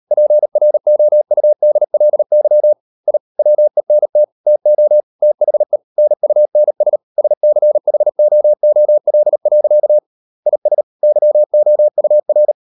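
Morse code, heard as a keyed tone, about 38 wpm, 600 Hz. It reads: PROUDLY I WENT TO THE NUNS SCHOOL. IS YOUR
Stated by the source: Morse code practice at 38 WPM, sending Ken Follett's novel Never in English, from CW med Gnister.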